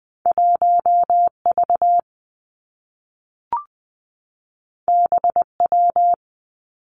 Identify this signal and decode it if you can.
Morse code, heard as a keyed tone, about 20 wpm, 700 Hz. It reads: 1 V B W
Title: Two Challenging Letter-Number Combinations 20wpm